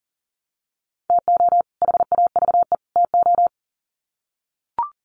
Morse code, heard as a keyed tone, about 40 wpm, 700 Hz. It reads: TO HAVE TO
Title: Sets of 3 Words 40wpm